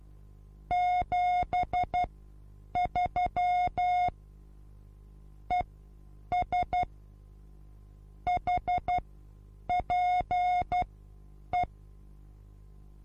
Morse code, heard as a keyed tone, about 12 wpm, spaced as for 8 wpm, 720 Hz. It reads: 73 ES HPE
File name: Onlinelehrgang / morsen / DJ4UF Morsen23